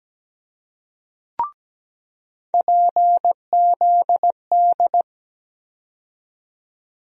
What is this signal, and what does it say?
Morse code, heard as a keyed tone, about 17 wpm, 700 Hz. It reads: PZD
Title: USA Call Sign Postfixes - 3 Letters 17wpm